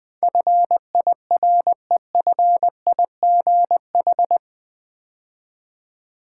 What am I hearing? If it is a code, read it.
Morse code, heard as a keyed tone, about 20 wpm, 700 Hz. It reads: FIREFIGH
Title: Build A Word - 4 Letters At A Time - 11-Letter Words - 20wpm